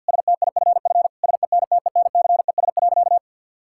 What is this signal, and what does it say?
Morse code, heard as a keyed tone, about 48 wpm, 700 Hz. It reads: STIFF SENTENCES.
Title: Pangram Challenge v2 - 50wpm